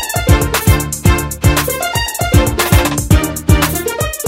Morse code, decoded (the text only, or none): none